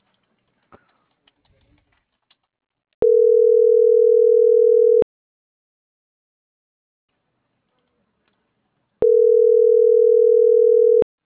Ringback tone (North America), 440 + 480 Hz.